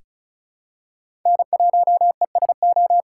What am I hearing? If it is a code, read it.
Morse code, heard as a keyed tone, about 34 wpm, 700 Hz. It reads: N1ESO
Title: USA Call Signs - 4x Character Spacing - 35wpm